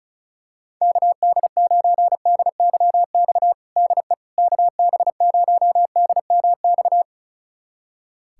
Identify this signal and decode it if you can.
Morse code, heard as a keyed tone, about 36 wpm, 700 Hz. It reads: KD9DYX DE KB0DMX
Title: USA Callsign DE Callsign 35wpm